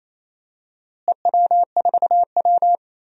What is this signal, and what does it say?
Morse code, heard as a keyed tone, about 27 wpm, 700 Hz. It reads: EW4W
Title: DXCC-100 Call Signs - 4 Characters - Repeated 3x - 28wpm